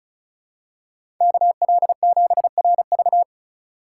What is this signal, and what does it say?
Morse code, heard as a keyed tone, about 36 wpm, 700 Hz. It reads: KL7RV
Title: USA Call Signs - 3x Character Spacing - 35wpm